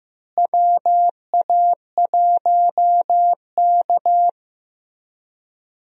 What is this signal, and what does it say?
Morse code, heard as a keyed tone, about 15 wpm, 700 Hz. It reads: WA1K
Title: USA Call Signs - 4 Characters - Repeated 3x - 15wpm